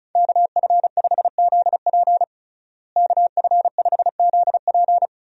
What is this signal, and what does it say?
Morse code, heard as a keyed tone, about 34 wpm, 700 Hz. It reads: KF5ZP KF5ZP
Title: USA Call Signs - 5 Characters - Repeated 3x - 35wpm